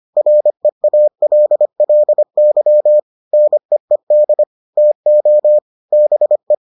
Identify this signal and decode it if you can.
Morse code, heard as a keyed tone, about 25 wpm, 600 Hz. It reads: REALLY NEED TO BE